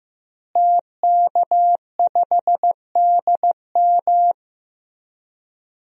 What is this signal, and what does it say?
Morse code, heard as a keyed tone, about 15 wpm, 700 Hz. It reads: TK5DM
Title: DXCC-100 Call Signs - 5 Characters - Repeated 2x - 15wpm